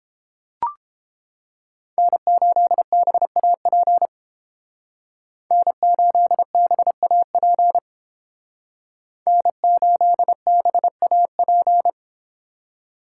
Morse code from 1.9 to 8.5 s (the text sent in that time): N8BAP N8BAP